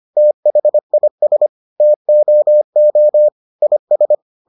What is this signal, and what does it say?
Morse code, heard as a keyed tone, about 25 wpm, 600 Hz. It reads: THIS TOO IS